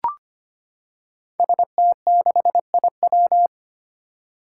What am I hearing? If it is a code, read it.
Morse code, heard as a keyed tone, about 25 wpm, 700 Hz. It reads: ST6IW
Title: Five Challenging Letter-Number Combinations 25wpm